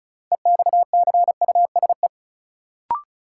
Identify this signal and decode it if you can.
Morse code, heard as a keyed tone, about 34 wpm, 700 Hz. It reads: EXCUSE
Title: Sentences from Top 700 Words - New Word Drill - 35wpm